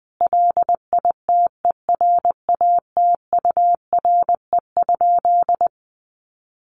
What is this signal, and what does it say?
Morse code, heard as a keyed tone, about 20 wpm, 700 Hz. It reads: LITERATURE?
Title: Sentences from Top 600 Words 20wpm